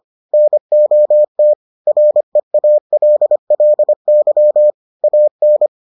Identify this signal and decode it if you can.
Morse code, heard as a keyed tone, about 25 wpm, 600 Hz. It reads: NOT REALLY AN